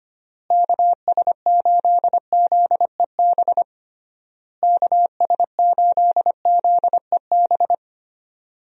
Morse code, heard as a keyed tone, about 25 wpm, 700 Hz. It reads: KS8ZEB KS8ZEB
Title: USA Call Signs - 6 Characters - Repeated 3x - 25wpm